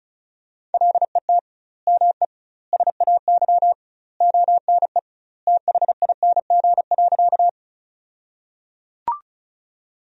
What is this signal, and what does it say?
Morse code, heard as a keyed tone, about 36 wpm, 700 Hz. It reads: LET ME SAY ONE THING.
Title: 2x Word-Spacing - Sentences from Top 300 Words - 35wpm